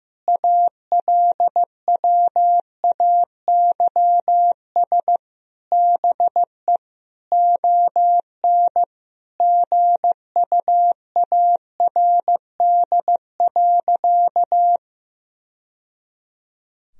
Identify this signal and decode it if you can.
Morse code, heard as a keyed tone, about 15 wpm, 700 Hz. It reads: ALWAYS BE ON GUARD.
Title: Sentences from Top 1000 Words - Review Part 4 - 15wpm